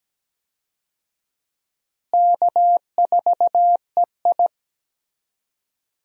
Morse code, heard as a keyed tone, about 17 wpm, 700 Hz. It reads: K4EI
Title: USA Call Signs - 4 Characters - Repeated 3x - 17wpm